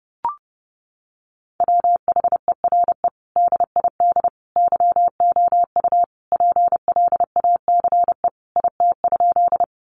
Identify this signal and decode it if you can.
Morse code, heard as a keyed tone, about 30 wpm, 700 Hz. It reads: WHERE DID YOU PLACE IT?
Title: Sentences from Top 200 Words 30wpm